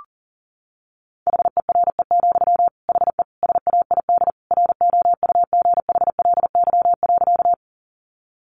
Morse code, heard as a keyed tone, about 37 wpm, 700 Hz. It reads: HERE, HE SAID ROUGHLY.
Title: Sentences from Top 1500 Words 40wpm